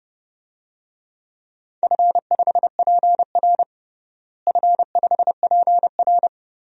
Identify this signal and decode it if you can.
Morse code, heard as a keyed tone, about 30 wpm, 700 Hz. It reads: F5PR F5PR
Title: DXCC-100 Call Signs - 4 Characters - Repeated 3x - 30wpm